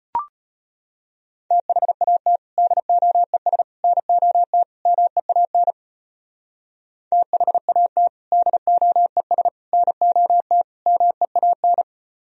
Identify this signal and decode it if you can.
Morse code, heard as a keyed tone, about 31 wpm, 700 Hz. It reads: THAT DOES NOT MEAN THAT DOES NOT MEAN
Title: Speed-Racing - Sets of 4 Words 25wpm